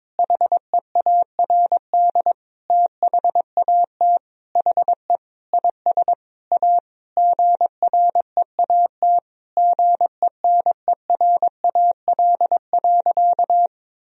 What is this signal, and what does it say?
Morse code, heard as a keyed tone, about 22 wpm, 700 Hz. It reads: HEARD THAT HE IS A GREAT GENERAL.
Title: Sentences from Top 1000 Words - Review Part 3 - 22wpm